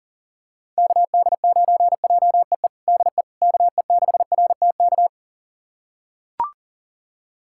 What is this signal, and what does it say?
Morse code, heard as a keyed tone, about 40 wpm, 700 Hz. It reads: KD9JEE DE KE6RTK
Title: USA Callsign DE Callsign - 6 Characters - 40wpm